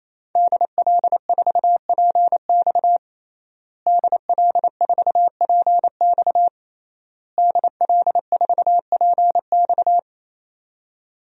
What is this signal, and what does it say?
Morse code, heard as a keyed tone, about 28 wpm, 700 Hz. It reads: DL4PX DL4PX DL4PX
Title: DXCC-100 Call Signs - 5 Characters - Repeated 3x - 28wpm